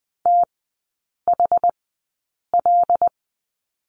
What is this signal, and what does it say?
Morse code, heard as a keyed tone, about 20 wpm, spaced as for 8 wpm, 700 Hz. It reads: THL
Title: 074 Lesson 13e - Letter L - Three Characters 20wpm